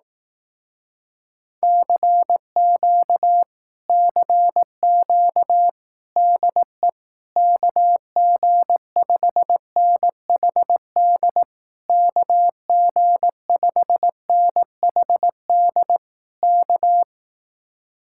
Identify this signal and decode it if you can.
Morse code, heard as a keyed tone, about 18 wpm, 700 Hz. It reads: CQ CQ DE KG5NHD KG5NHD K
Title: Speed-Racing Common QSO Phrases Encore - 15wpm